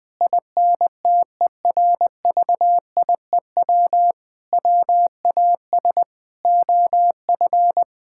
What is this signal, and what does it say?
Morse code, heard as a keyed tone, about 21 wpm, 700 Hz. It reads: INTERVIEW WAS OF
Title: Sentences from Top 1600 Words 20wpm